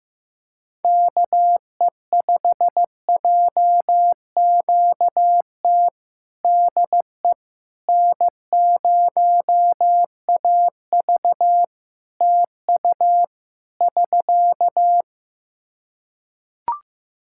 Morse code, heard as a keyed tone, about 15 wpm, 700 Hz. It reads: KE5JQT DE N0AV TU <SK>
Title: Practice QSO - Exchange 4 - Ending - 15wpm